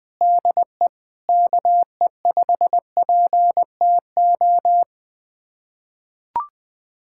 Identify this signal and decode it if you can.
Morse code, heard as a keyed tone, about 21 wpm, 700 Hz. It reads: DE KE5PTO
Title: USA Callsign DE Callsign - 6 Characters - 20wpm